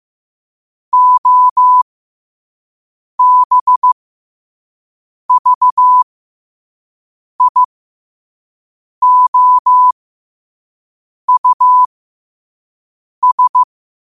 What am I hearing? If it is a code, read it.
Morse code, heard as a keyed tone, about 15 wpm, spaced as for 5 wpm, 1000 Hz. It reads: OBVIOUS
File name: qotdfiles / 5wpm